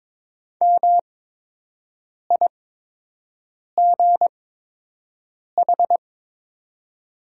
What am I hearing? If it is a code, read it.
Morse code, heard as a keyed tone, about 22 wpm, spaced as for 6 wpm, 700 Hz. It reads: MIGH
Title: 8x Character Spacing - Top 500 Words - 22wpm